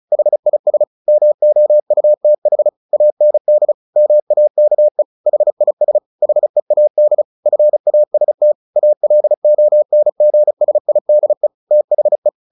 Morse code, heard as a keyed tone, about 34 wpm, 600 Hz. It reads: HIS MOUTH AND MAKE HIS HEAD FAST ALONGSIDE THE
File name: the old man and the sea 2 35wpm